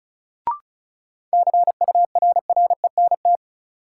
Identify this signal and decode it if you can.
Morse code, heard as a keyed tone, about 36 wpm, 700 Hz. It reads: CURRENT
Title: Sentences from Top 1000 Words - Review Part 4 - New Word Drill - 35wpm